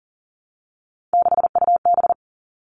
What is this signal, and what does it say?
Morse code, heard as a keyed tone, about 40 wpm, 700 Hz. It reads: 6UB
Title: All Cut Numbers - 3-Digit 40wpm